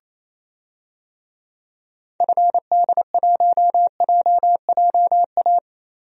Morse code, heard as a keyed tone, about 28 wpm, 700 Hz. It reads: FD1JJA
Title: DXCC-100 Call Signs - 6 Characters - Repeated 3x - 28wpm